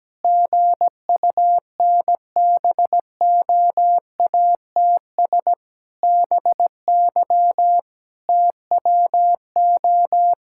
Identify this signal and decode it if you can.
Morse code, heard as a keyed tone, about 17 wpm, 700 Hz. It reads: GUNBOATS BY TWO
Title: Pangram Challenge v2 - 17wpm